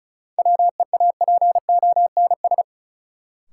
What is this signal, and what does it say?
Morse code, heard as a keyed tone, about 36 wpm, 700 Hz. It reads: WEAPONS
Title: Sets of 5 Words Encore - New Word Drill - 35wpm